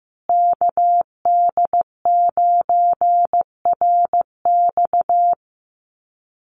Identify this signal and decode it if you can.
Morse code, heard as a keyed tone, about 15 wpm, 700 Hz. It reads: KD9RX
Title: USA Call Signs Repeated 3x 15wpm